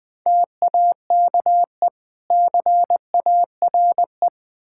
Morse code, heard as a keyed tone, about 21 wpm, 700 Hz. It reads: TAKE CARE
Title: Build A Phrase - 3-Word Phrases - 20wpm